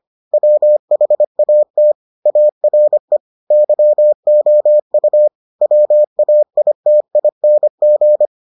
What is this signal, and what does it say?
Morse code, heard as a keyed tone, about 25 wpm, 600 Hz. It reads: WHAT ARE YOU WAITING